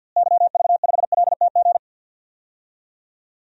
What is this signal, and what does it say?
Morse code, heard as a keyed tone, about 49 wpm, 700 Hz. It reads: YV5LTG
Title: DXCC-100 Call Signs - 6 Characters - Repeated 3x - 50wpm